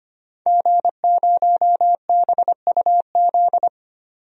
Morse code, heard as a keyed tone, about 25 wpm, 700 Hz. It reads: G0BUZ
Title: DXCC-100 Call Signs - 5 Characters - 25wpm